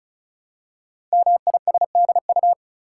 Morse code, heard as a keyed tone, about 36 wpm, 690 Hz. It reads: MISDU
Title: Code Groups - 5 Letters - 35wpm